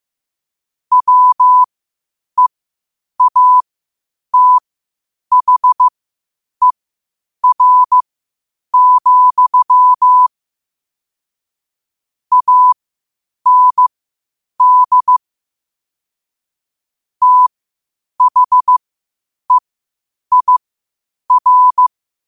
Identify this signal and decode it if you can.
Morse code, heard as a keyed tone, about 15 wpm, spaced as for 8 wpm, 1000 Hz. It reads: WEATHER, AND THEIR
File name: 8wpm